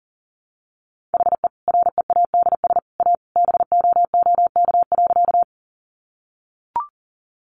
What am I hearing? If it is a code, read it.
Morse code, heard as a keyed tone, about 40 wpm, 700 Hz. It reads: HE READS A BOOK.
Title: Additional Easy Phrases 40wpm